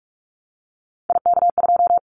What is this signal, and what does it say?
Morse code, heard as a keyed tone, about 44 wpm, 700 Hz. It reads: IK2
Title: Callsign Prefixes for Top 25 DXCC - 45wpm